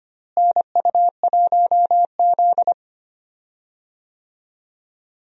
Morse code, heard as a keyed tone, about 25 wpm, 700 Hz. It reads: NU1Z
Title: USA Call Signs - 4 Characters - Repeated 2x - 25wpm